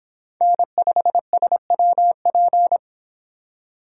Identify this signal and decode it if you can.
Morse code, heard as a keyed tone, about 25 wpm, 700 Hz. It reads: N5SWP